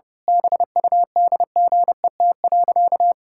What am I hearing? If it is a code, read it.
Morse code, heard as a keyed tone, about 31 wpm, 700 Hz. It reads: BUDGET.
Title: Pangram Challenge v2 - 30wpm